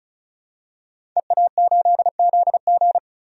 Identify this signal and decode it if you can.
Morse code, heard as a keyed tone, about 34 wpm, 700 Hz. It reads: EA8ZG
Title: DXCC-100 Call Signs - 5 Characters - 35wpm